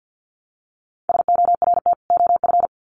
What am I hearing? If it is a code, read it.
Morse code, heard as a keyed tone, about 50 wpm, 700 Hz. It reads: SORT OF
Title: Sets of 2 Words 50wpm